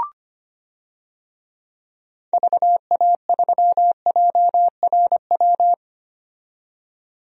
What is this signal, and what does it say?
Morse code, heard as a keyed tone, about 25 wpm, 700 Hz. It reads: VA3JRW
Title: DXCC-100 Call Signs - 6 Characters - 25wpm